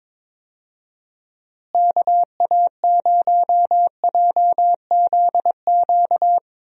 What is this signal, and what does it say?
Morse code, heard as a keyed tone, about 22 wpm, 700 Hz. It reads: KA0JZQ
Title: USA Call Signs - 6 Characters - Repeated 3x - 22wpm